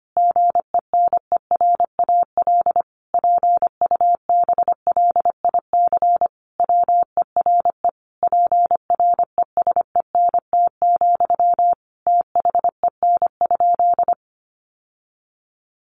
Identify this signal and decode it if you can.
Morse code, heard as a keyed tone, about 25 wpm, 700 Hz. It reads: GENERAL PUBLIC WERE PRESENT, THEN?
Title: Sentences from Top 800 Words 25wpm